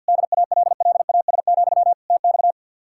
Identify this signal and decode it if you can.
Morse code, heard as a keyed tone, about 65 wpm, 700 Hz. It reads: DALLAS, TX